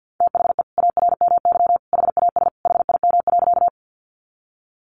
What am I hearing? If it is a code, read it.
Morse code, heard as a keyed tone, about 50 wpm, 700 Hz. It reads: THE ARMY HAS HIM.
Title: Sentences from Top 800 Words 50wpm